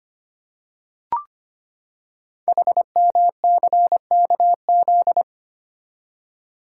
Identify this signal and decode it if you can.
Morse code, heard as a keyed tone, about 25 wpm, 700 Hz. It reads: HMCKZ
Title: Code Groups - 5 Letters - 25wpm